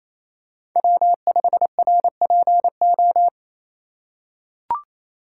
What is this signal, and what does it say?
Morse code, heard as a keyed tone, about 28 wpm, 700 Hz. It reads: W5RPO
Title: USA Call Signs - 5 Characters - 6x Character Spacing - 28wpm